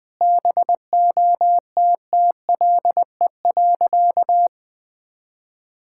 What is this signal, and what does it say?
Morse code, heard as a keyed tone, about 20 wpm, 700 Hz. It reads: BOTTLE.
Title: Sentences with 4-Word Phrases - 20wpm